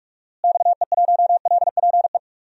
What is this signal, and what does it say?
Morse code, heard as a keyed tone, about 46 wpm, 700 Hz. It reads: XE1LPE
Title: DXCC-100 Call Signs - 6 Characters - Repeated 3x - 45wpm